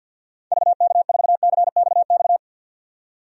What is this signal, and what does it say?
Morse code, heard as a keyed tone, about 52 wpm, 700 Hz. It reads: VK4CXX
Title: DXCC-100 Call Signs - 6 Characters - 50wpm